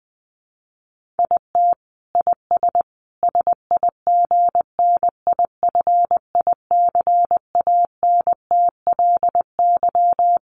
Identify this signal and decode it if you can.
Morse code, heard as a keyed tone, about 20 wpm, 700 Hz. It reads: IT IS SIGNIFICANTLY
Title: Sentences from Top 1500 Words 20wpm